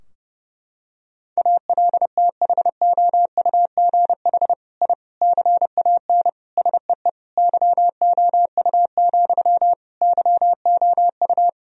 Morse code, heard as a keyed tone, about 32 wpm, 700 Hz. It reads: ALTHOUGH I CAN SEE YOU, YOU
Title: Sentences from Top 1000 Words - Review Part 3 - 30wpm